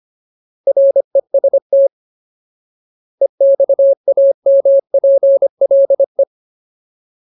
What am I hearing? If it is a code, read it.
Morse code, heard as a keyed tone, about 25 wpm, 550 Hz. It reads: REST EXAMPLE
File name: Ultimate Word List / 25WPM Ultimate